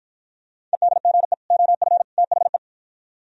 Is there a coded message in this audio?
Yes: Morse code from 0.7 to 2.6 s, heard as a keyed tone, about 52 wpm, 690 Hz. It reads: EDGE OF THE